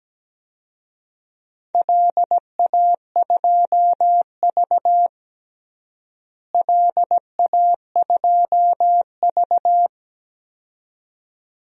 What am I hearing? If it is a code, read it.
Morse code, heard as a keyed tone, about 17 wpm, 700 Hz. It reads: LA2V LA2V
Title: DXCC-100 Call Signs - 4 Characters - Repeated 2x - 17wpm